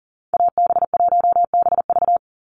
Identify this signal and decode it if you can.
Morse code, heard as a keyed tone, about 38 wpm, 690 Hz. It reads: AB1BV
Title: USA Call Signs 40wpm